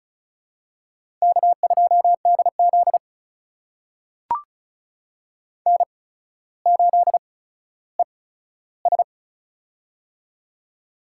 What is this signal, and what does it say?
Morse code, heard as a keyed tone, about 36 wpm, spaced as for 15 wpm, 700 Hz. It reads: K2DZ N8ES
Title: USA Call Signs - 4 Characters - 8x Character Spacing - 35wpm